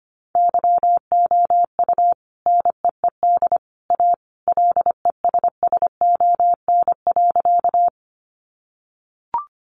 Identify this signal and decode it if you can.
Morse code, heard as a keyed tone, about 25 wpm, 700 Hz. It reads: YOU NEED A LESSON.
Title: Sentences from Top 1200 Words 25wpm